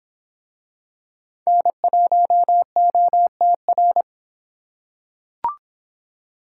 Morse code, heard as a keyed tone, about 24 wpm, 700 Hz. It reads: N1OTR